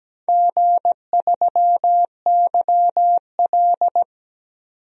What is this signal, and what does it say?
Morse code, heard as a keyed tone, about 17 wpm, 700 Hz. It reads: G3YL
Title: DXCC-100 Call Signs - 4 Characters - 17wpm